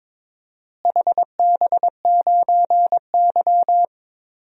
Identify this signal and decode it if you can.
Morse code, heard as a keyed tone, about 22 wpm, 700 Hz. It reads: HB9Y
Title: DXCC-100 Call Signs - 4 Characters - Repeated 3x - 22wpm